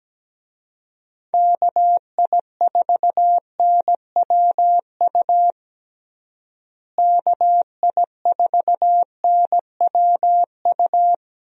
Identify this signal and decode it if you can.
Morse code, heard as a keyed tone, about 17 wpm, 700 Hz. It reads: KI4NWU KI4NWU
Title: USA Call Signs - 6 Characters - Repeated 3x - 17wpm